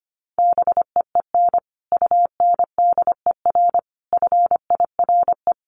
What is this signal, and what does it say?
Morse code, heard as a keyed tone, about 25 wpm, 700 Hz. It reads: BEEN UNDER FIRE
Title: Sentences from Top 700 Words 25wpm